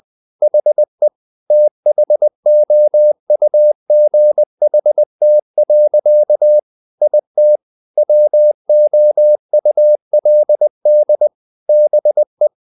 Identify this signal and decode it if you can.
Morse code, heard as a keyed tone, about 20 wpm, 600 Hz. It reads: HE THOUGHT. IT WOULD BE
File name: the old man and the sea / the old man and the sea part 1 20wpm